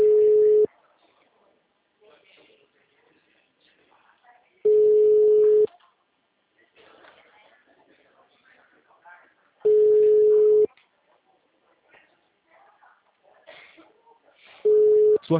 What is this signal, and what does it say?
Ringback tone (Europe, 425 Hz), with speech.